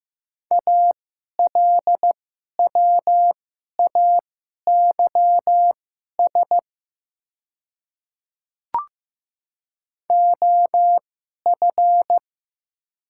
Morse code, heard as a keyed tone, about 15 wpm, spaced as for 11 wpm, 700 Hz. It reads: ALWAYS OF
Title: 2x Character Spacing - Top 300 Words - 15wpm